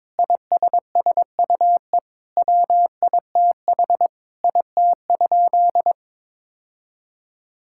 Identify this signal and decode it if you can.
Morse code, heard as a keyed tone, about 22 wpm, 700 Hz. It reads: ISSUE WITH IT?
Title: Sentences from Top 1000 Words - Review Part 6 - 22wpm